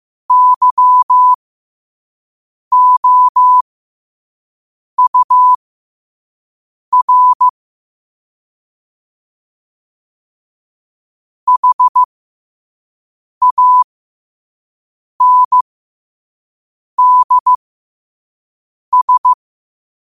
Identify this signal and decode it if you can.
Morse code, heard as a keyed tone, about 15 wpm, spaced as for 5 wpm, 1000 Hz. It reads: YOUR HANDS